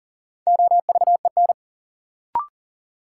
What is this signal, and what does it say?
Morse code, heard as a keyed tone, about 40 wpm, 700 Hz. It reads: OVEN
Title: Sentences from Top 1200 Words - New Word Drill Alt - 40wpm